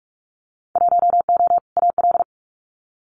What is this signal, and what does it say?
Morse code, heard as a keyed tone, about 46 wpm, 700 Hz. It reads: 1O AL
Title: Winter Field Day Exchanges Encore 45wpm